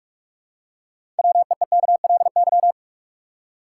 Morse code, heard as a keyed tone, about 44 wpm, 700 Hz. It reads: WEEKLY
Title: Six Letter Words - 45wpm